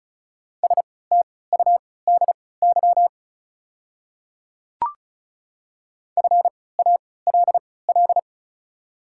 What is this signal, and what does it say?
Morse code, heard as a keyed tone, about 36 wpm, spaced as for 20 wpm, 690 Hz. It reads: STUDY FALL
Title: 3x Character Spacing - Top 500 Words - 35wpm